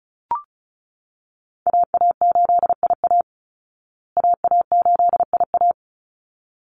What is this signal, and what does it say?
Morse code, heard as a keyed tone, about 34 wpm, 700 Hz. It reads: AA8IA AA8IA
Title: Call Signs Repeated 3x - 35wpm